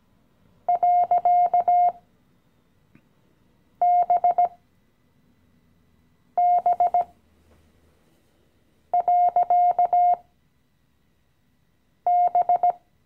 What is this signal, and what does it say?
Morse code, heard as a keyed tone, about 17 wpm, spaced as for 4 wpm, 710 Hz. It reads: .BB.B
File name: CW-Online-Kurs / Anfänger / Lektion 015